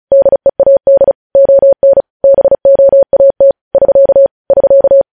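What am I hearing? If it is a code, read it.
Morse code, heard as a keyed tone, about 36 wpm, 550 Hz. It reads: DEAD ON BOAT <SK> <SK>
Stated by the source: News headlines sent in Morse at 35 WPM.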